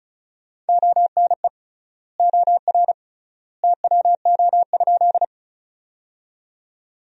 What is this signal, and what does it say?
Morse code, heard as a keyed tone, about 34 wpm, 700 Hz. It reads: ONE OR TWO?
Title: 3x Word-Spacing - Sentences from Top 100 Words - 35wpm